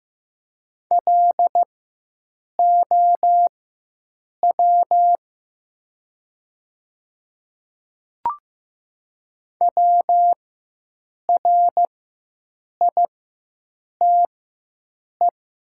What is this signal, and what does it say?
Morse code, heard as a keyed tone, about 15 wpm, spaced as for 7 wpm, 700 Hz. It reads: LOW WRITE
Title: 4x Character Spacing - Top 200 Words - 15wpm